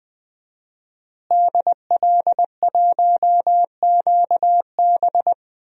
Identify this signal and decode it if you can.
Morse code, heard as a keyed tone, about 21 wpm, 700 Hz. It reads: DL1QB
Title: DXCC-100 Call Signs - 5 Characters - Repeated 2x - 20wpm